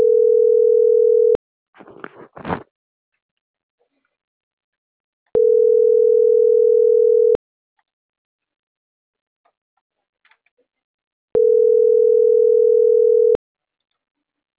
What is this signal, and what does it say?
Ringback tone (North America), 440 + 480 Hz.